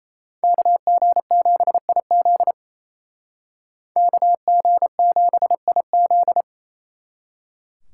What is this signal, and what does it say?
Morse code, heard as a keyed tone, about 35 wpm, 700 Hz. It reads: KG7IZ KG7IZ